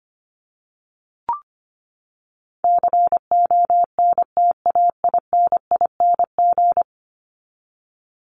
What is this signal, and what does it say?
Morse code, heard as a keyed tone, about 25 wpm, 700 Hz. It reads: CONTAINING
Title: ING Suffix - Top 100 Words with Base Words 25wpm